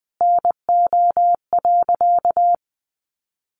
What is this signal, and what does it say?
Morse code, heard as a keyed tone, about 20 wpm, 700 Hz. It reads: NO.
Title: Sentences from Top 1200 Words 20wpm